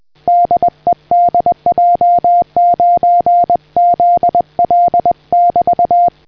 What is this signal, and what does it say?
Morse code, heard as a keyed tone, about 20 wpm, 700 Hz. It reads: DE DJ9ZL <BT>